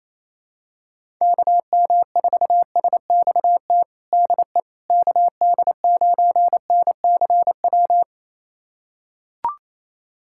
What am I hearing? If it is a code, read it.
Morse code, heard as a keyed tone, about 28 wpm, 700 Hz. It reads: KM4SXT DE KD9NCW
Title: USA Callsign DE Callsign - 6 Characters - 28wpm